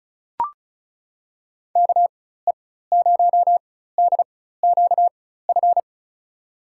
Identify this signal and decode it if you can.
Morse code, heard as a keyed tone, about 34 wpm, spaced as for 16 wpm, 700 Hz. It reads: KE0DQF
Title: USA Call Signs - 6 Characters - 4x Character Spacing - 35wpm